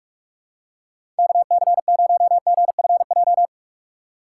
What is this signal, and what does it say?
Morse code, heard as a keyed tone, about 45 wpm, 700 Hz. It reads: KC0GFJ
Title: USA Call Signs - 6 Characters - 4x Character Spacing - 45wpm